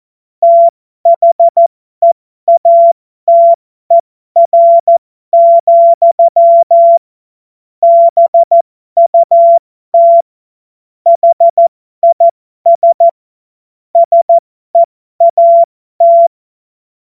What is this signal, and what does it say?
Morse code, heard as a keyed tone, about 14 wpm, spaced as for 12 wpm, 680 Hz. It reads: THEATER, BUT HIS SEAT
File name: Morsekurs / Witze